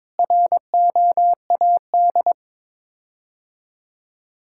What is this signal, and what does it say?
Morse code, heard as a keyed tone, about 22 wpm, 700 Hz. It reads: ROAD
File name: practice / 22wpm / Sets of 6 Words - 22wpm